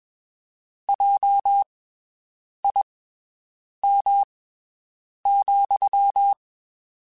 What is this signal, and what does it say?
Morse code, heard as a keyed tone, about 22 wpm, spaced as for 7 wpm, 800 Hz. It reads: JIM,
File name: (spaced) / QSOs Spaced-mp3s / WD0BGP 1 @20 sp